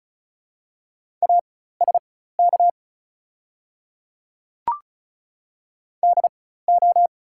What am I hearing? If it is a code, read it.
Morse code, heard as a keyed tone, about 36 wpm, spaced as for 16 wpm, 700 Hz. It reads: ASK DO
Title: 4x Character Spacing - Top 200 Words - 35wpm